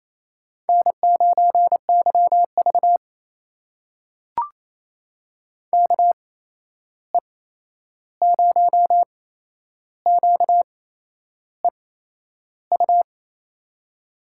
Morse code, heard as a keyed tone, about 28 wpm, spaced as for 11 wpm, 700 Hz. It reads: N9YV KE0QEU
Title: USA Call Signs - 8x Character Spacing - 28wpm